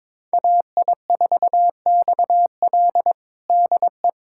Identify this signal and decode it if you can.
Morse code, heard as a keyed tone, about 22 wpm, 700 Hz. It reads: AI4XL DE
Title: USA Callsign DE Callsign - 5 Characters - 22wpm